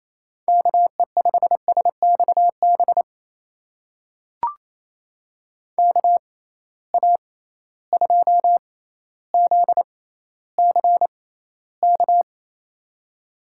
Morse code, heard as a keyed tone, about 27 wpm, spaced as for 14 wpm, 700 Hz. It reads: KE5SXB KA2ZCK
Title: USA Call Signs - 6 Characters - 6x Character Spacing - 28wpm